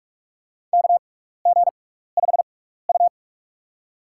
Morse code, heard as a keyed tone, about 46 wpm, spaced as for 15 wpm, 700 Hz. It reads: KG5U
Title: USA Call Signs - 6x Character Spacing - 45wpm